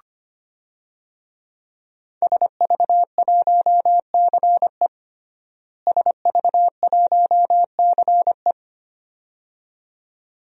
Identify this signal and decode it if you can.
Morse code, heard as a keyed tone, about 25 wpm, 700 Hz. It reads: SV1CE SV1CE
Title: DXCC-100 Call Signs - 5 Characters - Repeated 2x - 25wpm